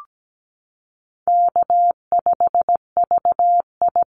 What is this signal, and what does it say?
Morse code, heard as a keyed tone, about 17 wpm, 700 Hz. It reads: K5VI
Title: Easy Call Signs - 17wpm